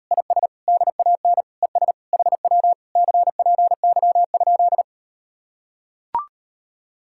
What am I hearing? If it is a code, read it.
Morse code, heard as a keyed tone, about 38 wpm, 700 Hz. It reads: IS DAN ES HW CPY?